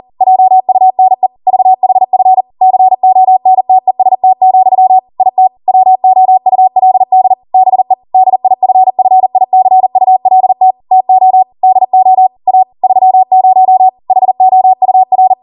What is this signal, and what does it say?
Morse code, heard as a keyed tone, about 40 wpm, 750 Hz. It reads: JUNE VHF CONTEST, IT WOULD BE DIFFICULT TO DO A 30 HOUR